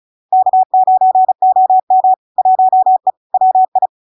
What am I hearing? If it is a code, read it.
Morse code, heard as a keyed tone, about 35 wpm, 750 Hz. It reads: K9OM 1E WI
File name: LARK-FD-2022-Log-in-CW-35wpm-750-Hz